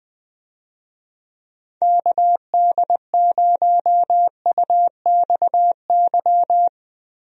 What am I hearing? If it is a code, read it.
Morse code, heard as a keyed tone, about 20 wpm, 700 Hz. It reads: KD0UXY
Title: USA Call Signs - 6 Characters - Repeated 3x - 20wpm